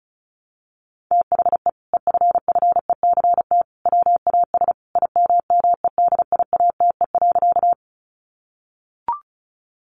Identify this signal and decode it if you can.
Morse code, heard as a keyed tone, about 34 wpm, 700 Hz. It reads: THE EFFECT WAS IMMEDIATE.
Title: Sentences from Top 1200 Words 35wpm